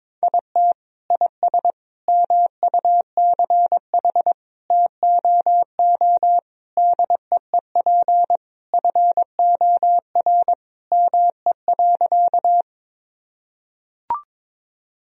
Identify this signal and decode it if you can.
Morse code, heard as a keyed tone, about 22 wpm, 700 Hz. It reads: IT IS MUCH TOO DEEP FOR ME.
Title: Sentences from Top 1000 Words - Review Part 3 - 22wpm